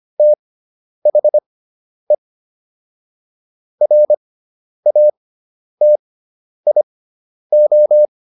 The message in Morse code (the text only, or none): THE RATIO